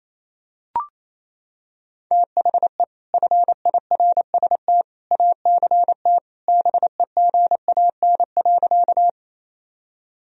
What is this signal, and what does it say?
Morse code, heard as a keyed tone, about 27 wpm, 700 Hz. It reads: THE FIRST ACT BEGAN.
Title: Sentences from Top 1000 Words - Review Part 1 - 28wpm